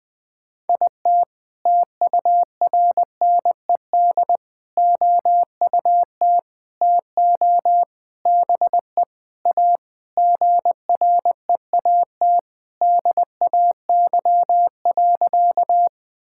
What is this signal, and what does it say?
Morse code, heard as a keyed tone, about 21 wpm, 700 Hz. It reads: IT TURNED OUT TO BE A GREAT DAY.
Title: Sentences with 4-Word Phrases - Spoken Beginning - 20wpm